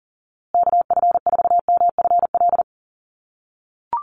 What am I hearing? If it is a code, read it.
Morse code, heard as a keyed tone, about 40 wpm, 700 Hz. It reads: KF4MFL
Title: USA Call Signs Repeated 2x 40wpm